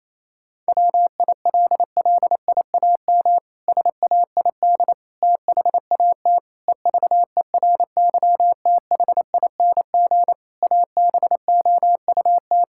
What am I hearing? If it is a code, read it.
Morse code, heard as a keyed tone, about 28 wpm, 700 Hz. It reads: WILLIAM SAID THAT EVERYTHING ABOUT
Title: Pangram Challenge v2 - 28wpm